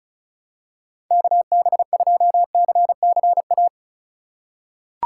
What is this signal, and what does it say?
Morse code, heard as a keyed tone, about 36 wpm, 700 Hz. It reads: KB2CCA
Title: USA Call Signs - 6 Characters - 4x Character Spacing - 35wpm